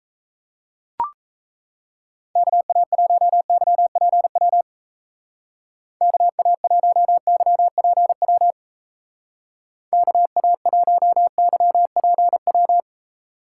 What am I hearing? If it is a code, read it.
Morse code, heard as a keyed tone, about 33 wpm, 700 Hz. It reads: KA1YPW KA1YPW KA1YPW
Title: Speed Racing - USA 2x3 Call Signs 28wpm